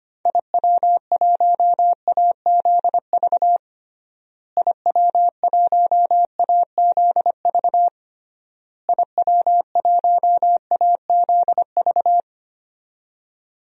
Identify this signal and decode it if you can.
Morse code, heard as a keyed tone, about 25 wpm, 700 Hz. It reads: IW1AZV IW1AZV IW1AZV
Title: DXCC-100 Call Signs - 6 Characters - Repeated 3x - 25wpm